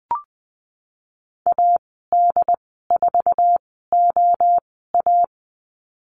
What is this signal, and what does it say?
Morse code, heard as a keyed tone, about 20 wpm, spaced as for 15 wpm, 700 Hz. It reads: AD4OA
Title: 088 Lesson 15g - Number 5 - Call Signs 20wpm